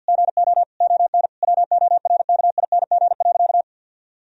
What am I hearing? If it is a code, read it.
Morse code, heard as a keyed tone, about 50 wpm, 700 Hz. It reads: GO ON WORKING.
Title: Sentences from Top 1000 Words - Review Part 4 - 50wpm